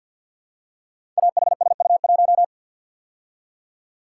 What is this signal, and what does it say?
Morse code, heard as a keyed tone, about 51 wpm, 700 Hz. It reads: AHSU1
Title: Code Groups - 5 Letter:Numbers - 50wpm